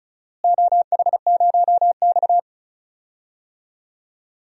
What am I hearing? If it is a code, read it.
Morse code, heard as a keyed tone, about 34 wpm, 700 Hz. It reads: OH0X
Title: DXCC-100 Call Signs - 4 Characters - Repeated 3x - 35wpm